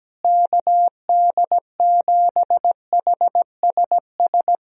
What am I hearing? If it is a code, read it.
Morse code, heard as a keyed tone, about 17 wpm, 700 Hz. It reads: KD7HSS
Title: USA Call Signs - 6 Characters - Repeated 2x - 17wpm